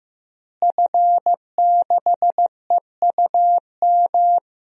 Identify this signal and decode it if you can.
Morse code, heard as a keyed tone, about 15 wpm, 700 Hz. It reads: F6EUM
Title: DXCC-100 Call Signs - 5 Characters - Repeated 2x - 15wpm